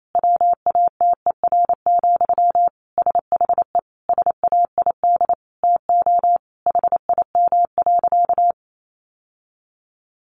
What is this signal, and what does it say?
Morse code, heard as a keyed tone, about 28 wpm, 700 Hz. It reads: WATER, SHE SAID TO HIM.
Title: Sentences from Top 800 Words 28wpm